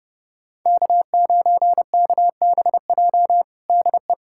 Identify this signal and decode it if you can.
Morse code, heard as a keyed tone, about 32 wpm, 700 Hz. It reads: K9KBJ DE
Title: USA Callsign DE Callsign - 5 Characters - 30wpm